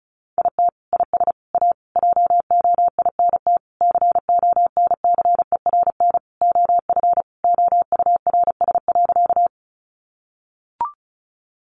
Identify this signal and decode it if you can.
Morse code, heard as a keyed tone, about 34 wpm, 700 Hz. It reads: IT IS A JOINT CONCERN OF OURS.
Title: Sentences from Top 1300 Words 35wpm